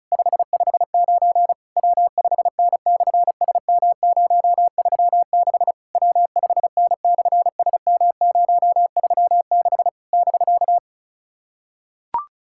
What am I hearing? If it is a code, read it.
Morse code, heard as a keyed tone, about 34 wpm, 700 Hz. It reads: 559 W5N/SM036 W5N/SM036 <BK>
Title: SOTA QSO Practice 35wpm